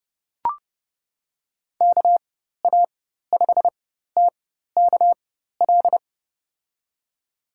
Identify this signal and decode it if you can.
Morse code, heard as a keyed tone, about 30 wpm, spaced as for 14 wpm, 700 Hz. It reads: KA5TKL
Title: USA Call Signs - 4x Character Spacing - 30wpm